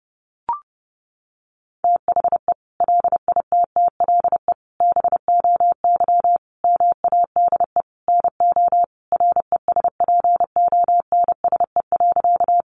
Morse code, heard as a keyed tone, about 30 wpm, 700 Hz. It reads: THE LITTLE BOY MADE NO RESPONSE.
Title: Sentences from Top 1400 Words 30wpm